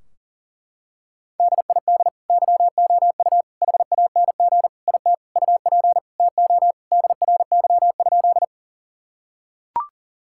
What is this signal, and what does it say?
Morse code, heard as a keyed tone, about 40 wpm, 700 Hz. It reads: DID YOU HANG IT UP TO DRY?
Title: Sentences from Top 1600 Words 40wpm